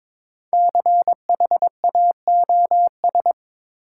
Morse code, heard as a keyed tone, about 22 wpm, 700 Hz. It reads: CHAOS
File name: 5 Letter Words - First 3 Letters Given - 22wpm